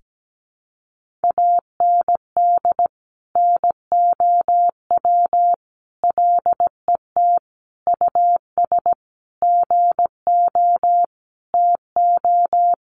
Morse code, heard as a keyed tone, about 17 wpm, 700 Hz. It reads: AND NOW LET US GO TO
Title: Sentences from Top 1500 Words 17wpm